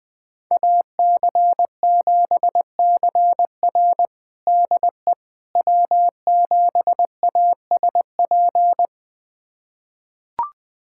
Morse code, heard as a keyed tone, about 20 wpm, 700 Hz. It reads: AC7CR DE W7ASP
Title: USA Callsign DE Callsign - 5 Characters - 20wpm